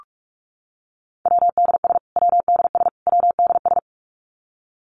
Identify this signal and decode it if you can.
Morse code, heard as a keyed tone, about 44 wpm, 700 Hz. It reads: WDS WDS WDS
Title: Abbreviations Repeated 3x - 45wpm